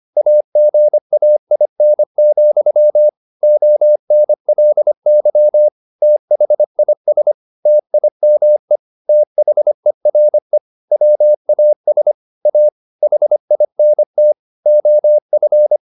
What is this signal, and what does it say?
Morse code, heard as a keyed tone, about 25 wpm, 600 Hz. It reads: AGAIN, ONLY THIS TIME THERE WAS A HINT OF